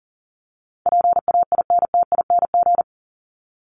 Morse code, heard as a keyed tone, about 40 wpm, 700 Hz. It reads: PAINTING
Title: Top 751-1000 Words 40wpm